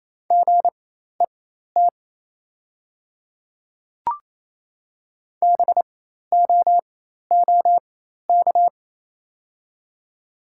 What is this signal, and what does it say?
Morse code, heard as a keyed tone, about 28 wpm, spaced as for 13 wpm, 690 Hz. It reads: GET BOOK
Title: 4x Character Spacing - Top 300 Words - 28wpm